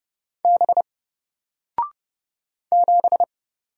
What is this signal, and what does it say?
Morse code, heard as a keyed tone, about 30 wpm, 700 Hz. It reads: B 7
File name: Single-Letter-Number-30wpm